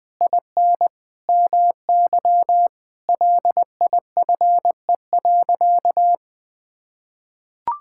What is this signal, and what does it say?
Morse code, heard as a keyed tone, about 21 wpm, 700 Hz. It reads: IN MY LIFE.
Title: Sentences from Top 1000 Words - Review Part 5 - 20wpm